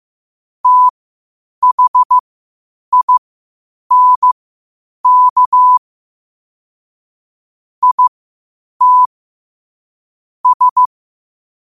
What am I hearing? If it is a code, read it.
Morse code, heard as a keyed tone, about 15 wpm, spaced as for 8 wpm, 1000 Hz. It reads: THINK IT S
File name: qotdfiles / 8wpm